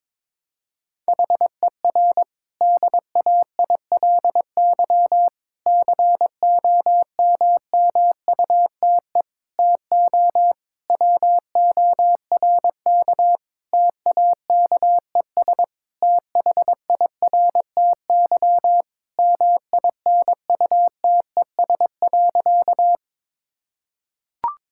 Morse code, heard as a keyed tone, about 22 wpm, 700 Hz. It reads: HER DAILY COMMUTE TO WORK TAKES THIRTY MINUTES.
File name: Difficult Sentences with Hints - 22wpm